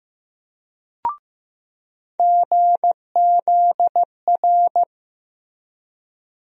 Morse code, heard as a keyed tone, about 15 wpm, 700 Hz. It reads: GZR
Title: USA Call Sign Postfixes - 3 Letters 15wpm